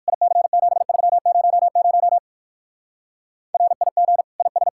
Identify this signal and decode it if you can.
Morse code, heard as a keyed tone, about 56 wpm, 700 Hz. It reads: IC7300 RIG IS